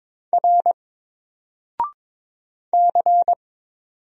Morse code, heard as a keyed tone, about 22 wpm, 700 Hz. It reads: R C
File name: Single-Letter-Number-22wpm